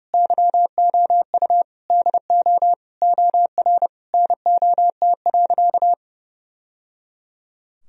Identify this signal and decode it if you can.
Morse code, heard as a keyed tone, about 31 wpm, 700 Hz. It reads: YOU DO OR NOT.
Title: Sentences from Top 1000 Words - Review Part 3 - 30wpm